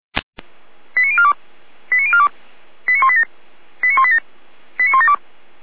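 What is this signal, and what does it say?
Five-tone selective call (ZVEI), heard as quick runs of five tones, tones of about 70 ms. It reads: 89042 89042 89187 89187 89172
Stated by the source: Five-tone ZVEI selective call received on SDR.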